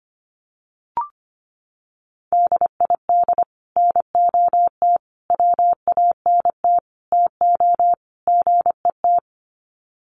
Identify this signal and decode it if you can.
Morse code, heard as a keyed tone, about 25 wpm, 700 Hz. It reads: DID NOT WANT TO GET
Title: Sets of 5 Words Encore2 25wpm